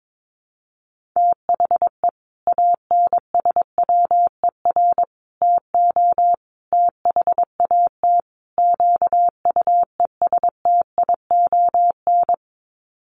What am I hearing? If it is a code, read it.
Morse code, heard as a keyed tone, about 22 wpm, 700 Hz. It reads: THE ANSWER TO THAT QUESTION
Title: Sets of 5 Words Encore2 22wpm